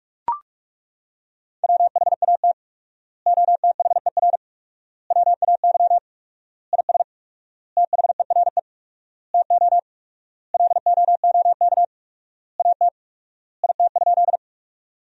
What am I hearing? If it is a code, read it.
Morse code, heard as a keyed tone, about 42 wpm, 700 Hz. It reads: WHAT OTHER WAY IS THERE TO LOOK AT IT?
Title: 4x Word-Spacing - Sentences from Top 100 Words - 45wpm